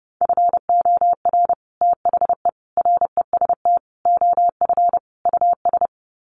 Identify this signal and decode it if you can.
Morse code, heard as a keyed tone, about 30 wpm, 700 Hz. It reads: FOR THE REST OF US
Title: Sets of 5 Words Encore 30wpm